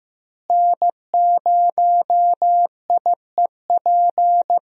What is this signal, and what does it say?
Morse code, heard as a keyed tone, about 15 wpm, 700 Hz. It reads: N0IEP
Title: USA Call Signs - 5 Characters - Repeated 3x - 15wpm